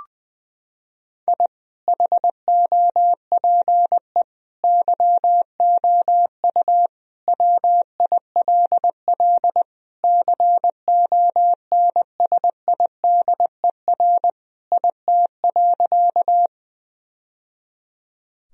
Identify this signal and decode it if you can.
Morse code, heard as a keyed tone, about 21 wpm, 700 Hz. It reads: I HOPE YOU WILL CONSIDER IT.
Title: Sentences from Top 1000 Words - Review Part 4 - 20wpm